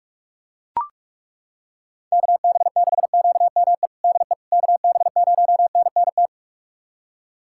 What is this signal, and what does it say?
Morse code, heard as a keyed tone, about 41 wpm, 700 Hz. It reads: KB6QME DE KB0NNT
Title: USA Callsign DE Callsign - 6 Characters - 45wpm